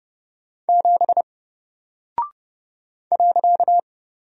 Morse code, heard as a keyed tone, about 30 wpm, 700 Hz. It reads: 7 .
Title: Single Character (Letter:Number:Punctuation:Prosign) Alt 30wpm